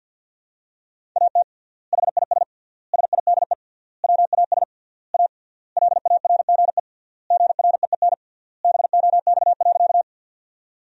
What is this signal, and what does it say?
Morse code, heard as a keyed tone, about 51 wpm, 700 Hz. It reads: AT HIS SIDE WAS A LARGE GREEN BOX.
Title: 3x Word-Spacing - Sentences from Top 500 Words - 50wpm